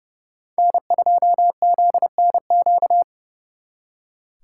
Morse code, heard as a keyed tone, about 37 wpm, 700 Hz. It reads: N2ZNQ
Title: Speed Racing - USA Call Signs 30wpm